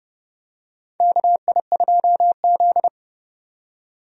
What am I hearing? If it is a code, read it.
Morse code, heard as a keyed tone, about 29 wpm, 700 Hz. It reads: KI2Z
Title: USA Call Signs - 4 Characters - 30wpm